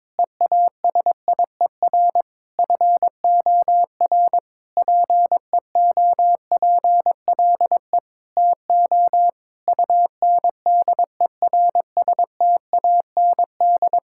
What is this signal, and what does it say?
Morse code, heard as a keyed tone, about 22 wpm, 700 Hz. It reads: EASIER FOR PEOPLE TO UNDERSTAND
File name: Sentences with 7-Word Phrases - 22wpm